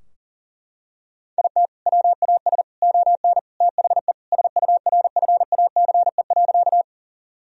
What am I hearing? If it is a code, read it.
Morse code, heard as a keyed tone, about 40 wpm, 700 Hz. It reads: IT WAS ON THE SURFACE.
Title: Sentences from Top 1000 Words - Review Part 3 - 40wpm